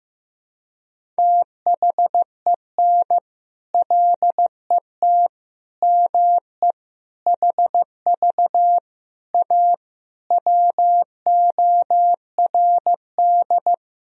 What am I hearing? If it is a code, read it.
Morse code, heard as a keyed tone, about 15 wpm, 700 Hz. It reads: THEN LET ME HV A WORD
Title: Sentences from Top 300 Words and Abbreviations 15wpm